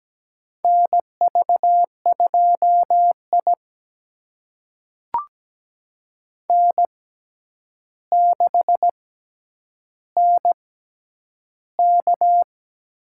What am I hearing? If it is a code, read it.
Morse code, heard as a keyed tone, about 17 wpm, spaced as for 9 wpm, 700 Hz. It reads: NV2I N6NK
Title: USA Call Signs - 4 Characters - 6x Character Spacing - 17wpm